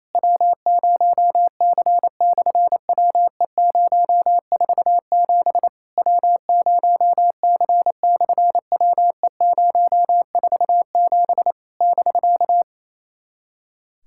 Morse code, heard as a keyed tone, about 28 wpm, 700 Hz. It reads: W0C/WE047 W0C/WE047 <BK>
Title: SOTA QSO Practice 28wpm